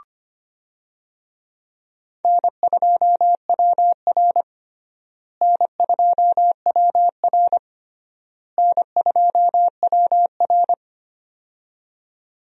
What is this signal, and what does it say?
Morse code, heard as a keyed tone, about 25 wpm, 700 Hz. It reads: N2WR N2WR N2WR
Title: USA Call Signs - 4 Characters - Repeated 3x - 25wpm